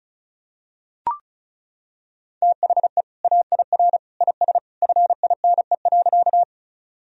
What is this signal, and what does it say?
Morse code, heard as a keyed tone, about 34 wpm, 700 Hz. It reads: THE AIR IS FINE.
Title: Sentences from Top 1000 Words - Review Part 1 - 35wpm